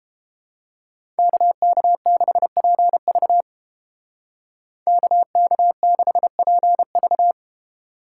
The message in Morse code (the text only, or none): KK6PV KK6PV